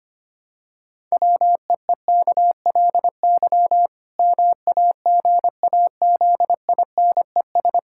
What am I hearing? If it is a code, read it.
Morse code, heard as a keyed tone, about 25 wpm, 700 Hz. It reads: WEEKLY MAGAZINES
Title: Pangram Challenge v2 - 25wpm